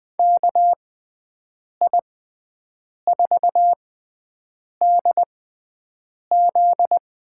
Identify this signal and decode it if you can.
Morse code, heard as a keyed tone, about 21 wpm, spaced as for 7 wpm, 700 Hz. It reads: KI4DZ
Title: USA Call Signs - 6x Character Spacing - 20wpm